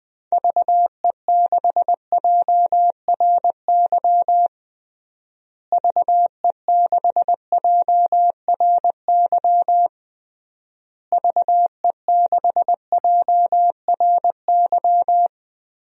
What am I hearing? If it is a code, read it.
Morse code, heard as a keyed tone, about 20 wpm, 700 Hz. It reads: VE6JRY VE6JRY VE6JRY
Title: DXCC-100 Call Signs - 6 Characters - Repeated 3x - 20wpm